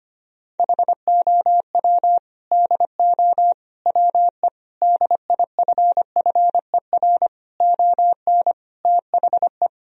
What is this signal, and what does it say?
Morse code, heard as a keyed tone, about 25 wpm, 700 Hz. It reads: HOW DO WE DIFFER ON THE
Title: Sentences from Top 1000 Words - Review Part 1 - 25wpm